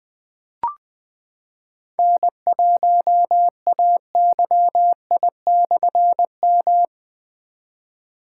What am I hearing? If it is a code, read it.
Morse code, heard as a keyed tone, about 20 wpm, 700 Hz. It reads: N1AYI/M
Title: Difficult Call Signs 20wpm